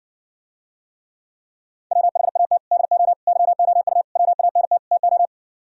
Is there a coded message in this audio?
Yes: Morse code from 1.9 to 5.3 s, heard as a keyed tone, about 69 wpm, 700 Hz. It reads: WHAT DO YOU WANT TO